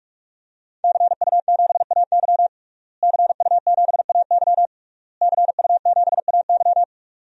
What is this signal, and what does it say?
Morse code, heard as a keyed tone, about 46 wpm, 700 Hz. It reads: CU7AY CU7AY CU7AY
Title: DXCC-100 Call Signs - 5 Characters - Repeated 3x - 45wpm